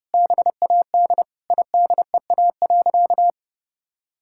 Morse code, heard as a keyed tone, about 32 wpm, 700 Hz. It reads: BAD IDEA.
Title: Sentences from Top 1000 Words - Review Part 5 - 30wpm